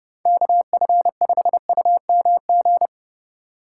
Morse code, heard as a keyed tone, about 30 wpm, 700 Hz. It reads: KF5UMG
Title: USA Call Signs - 6 Characters - 2x Character Spacing - 30wpm